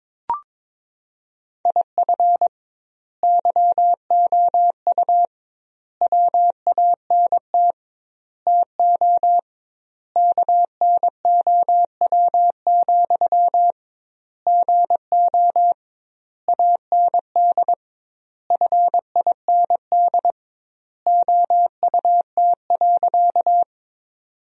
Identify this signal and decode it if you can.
Morse code, heard as a keyed tone, about 22 wpm, 700 Hz. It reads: IF YOU WANT TO KNOW, GO AND FIND OUT.
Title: 2x Word-Spacing - Sentences from Top 200 Words - 22wpm